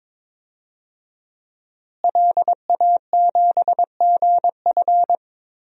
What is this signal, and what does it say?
Morse code, heard as a keyed tone, about 21 wpm, 700 Hz. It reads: LA7GF
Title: DXCC-100 Call Signs - 5 Characters - Repeated 3x - 22wpm